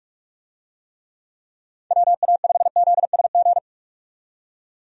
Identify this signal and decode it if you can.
Morse code, heard as a keyed tone, about 46 wpm, 700 Hz. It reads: WA5ZSG
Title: USA Call Signs - 6 Characters - 45wpm